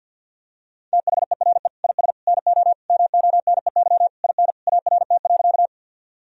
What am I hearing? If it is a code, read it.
Morse code, heard as a keyed tone, about 50 wpm, 700 Hz. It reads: THERE IS NO MONEY IN ART.
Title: Sentences from Top 1000 Words - Review Part 3 - 50wpm